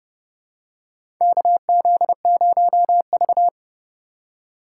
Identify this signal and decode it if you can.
Morse code, heard as a keyed tone, about 30 wpm, 700 Hz. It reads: KZ0V
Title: USA Call Signs - 4 Characters - 3x Character Spacing - 30wpm